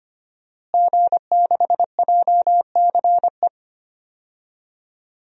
Morse code, heard as a keyed tone, about 25 wpm, 700 Hz. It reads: G6JCE
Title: DXCC-100 Call Signs - 5 Characters - Repeated 2x - 25wpm